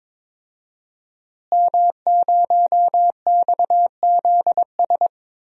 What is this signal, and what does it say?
Morse code, heard as a keyed tone, about 22 wpm, 700 Hz. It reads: M0XZS
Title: DXCC-100 Call Signs - 5 Characters - Repeated 2x - 22wpm